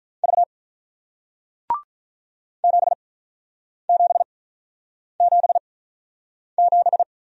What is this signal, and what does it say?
Morse code, heard as a keyed tone, about 47 wpm, spaced as for 9 wpm, 700 Hz. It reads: V 7777